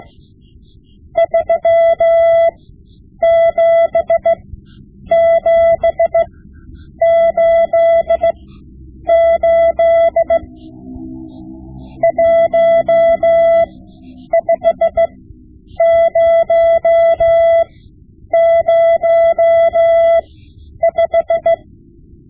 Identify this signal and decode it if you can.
Morse code, heard as a keyed tone, about 16 wpm, spaced as for 9 wpm, 670 Hz. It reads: 37788 15005
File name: Morse Lessons / Morse Lessons / Tape-5B